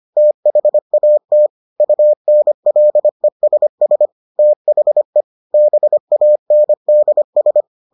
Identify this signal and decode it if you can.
Morse code, heard as a keyed tone, about 25 wpm, 600 Hz. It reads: THAT UNLESS THE BANDS